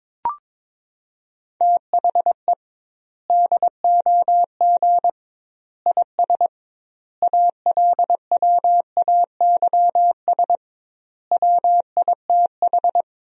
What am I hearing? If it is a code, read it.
Morse code, heard as a keyed tone, about 22 wpm, 700 Hz. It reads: THE DOG IS ALWAYS WITH
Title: 2x Word-Spacing - Sentences from Top 400 Words - 22wpm